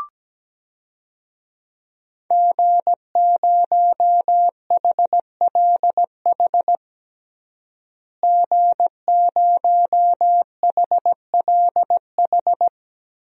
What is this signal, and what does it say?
Morse code, heard as a keyed tone, about 17 wpm, 700 Hz. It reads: G0HLH G0HLH
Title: DXCC-100 Call Signs - 5 Characters - Repeated 3x - 17wpm